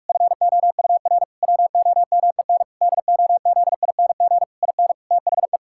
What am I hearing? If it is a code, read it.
Morse code, heard as a keyed tone, about 46 wpm, 700 Hz. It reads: FOUR WOMEN DOZING IN THE
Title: Pangram Challenge v2 - 45wpm